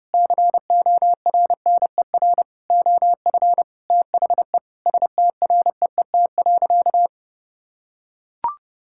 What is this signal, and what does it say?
Morse code, heard as a keyed tone, about 30 wpm, 700 Hz. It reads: CORNER OF THE STREET.
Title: Sentences from Top 1600 Words 30wpm